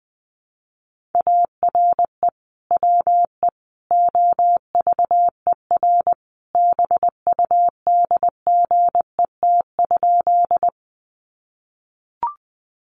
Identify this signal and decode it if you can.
Morse code, heard as a keyed tone, about 20 wpm, 700 Hz. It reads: ARE WE OVER BUDGET?
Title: Sentences from Top 1200 Words 20wpm